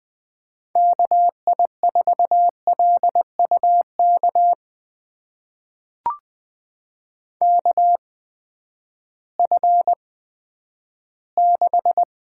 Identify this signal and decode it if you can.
Morse code, heard as a keyed tone, about 20 wpm, spaced as for 11 wpm, 700 Hz. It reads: KI4LUK KF6
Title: USA Call Signs - 8x Character Spacing - 20wpm